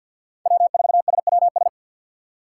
Morse code, heard as a keyed tone, about 47 wpm, 700 Hz. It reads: W4SWS
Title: USA Call Signs - 5 Characters - 6x Character Spacing - 50wpm